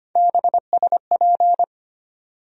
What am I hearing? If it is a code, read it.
Morse code, heard as a keyed tone, about 25 wpm, 700 Hz. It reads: BSP
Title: USA Call Sign Postfixes - 3 Letters 25wpm